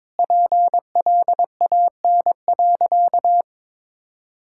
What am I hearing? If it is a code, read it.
Morse code, heard as a keyed tone, about 22 wpm, 700 Hz. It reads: PLAN.
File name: practice / 22wpm / Sentences with 8-Word Phrases - 22wpm